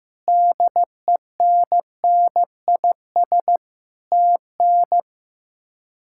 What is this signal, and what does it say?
Morse code, heard as a keyed tone, about 15 wpm, 700 Hz. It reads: DENNIS TN
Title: K1USN SST QSO Practice - 15wpm